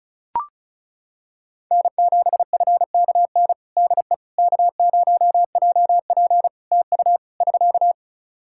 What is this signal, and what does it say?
Morse code, heard as a keyed tone, about 36 wpm, 700 Hz. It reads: N7FKN DE K0JP TU <SK>
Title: Practice QSO - Exchange 4 - Ending - 35wpm